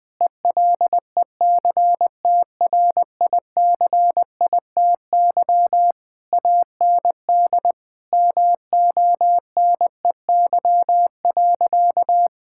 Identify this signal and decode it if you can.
Morse code, heard as a keyed tone, about 20 wpm, 700 Hz. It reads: ELECTRICITY AND MONEY.
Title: Difficult Sentences with Hints - 20wpm